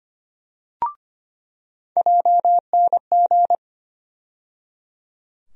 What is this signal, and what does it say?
Morse code, heard as a keyed tone, about 25 wpm, 700 Hz. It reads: JNG
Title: USA Call Sign Postfixes - 3 Letters 25wpm